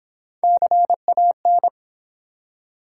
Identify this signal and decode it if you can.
Morse code, heard as a keyed tone, about 23 wpm, 700 Hz. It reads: CAN